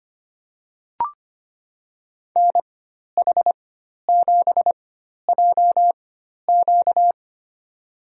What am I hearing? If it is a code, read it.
Morse code, heard as a keyed tone, about 25 wpm, spaced as for 12 wpm, 700 Hz. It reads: NH7JQ
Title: USA Call Signs - 5 Characters - 4x Character Spacing - 25wpm